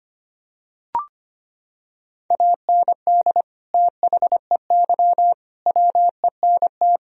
Morse code, heard as a keyed tone, about 25 wpm, 700 Hz. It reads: AND THEY WENT
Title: Sentences from Top 1000 Words - Review Part 2 - 25wpm